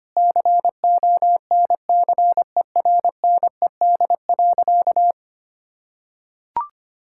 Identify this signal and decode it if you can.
Morse code, heard as a keyed tone, about 25 wpm, 700 Hz. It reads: CONCERNED.
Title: Sentences from Top 1000 Words - Review Part 6 - 25wpm